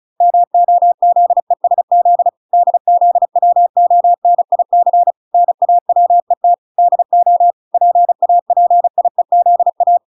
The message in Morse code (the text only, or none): MOZESZ DZWONIC NAWET DO PAPIEZA